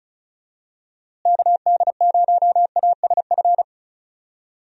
Morse code, heard as a keyed tone, about 36 wpm, 700 Hz. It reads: KD0ASF
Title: USA Call Signs - 6 Characters - Repeated 3x - 35wpm